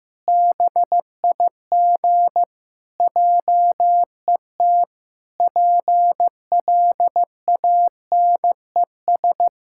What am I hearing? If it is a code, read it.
Morse code, heard as a keyed tone, about 16 wpm, 700 Hz. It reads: BIG JET PLANES
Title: Pangram Challenge v2 - 15wpm